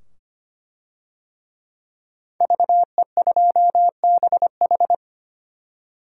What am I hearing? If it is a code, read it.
Morse code, heard as a keyed tone, about 25 wpm, 700 Hz. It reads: VE2BH
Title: DXCC-100 Call Signs - 5 Characters - 25wpm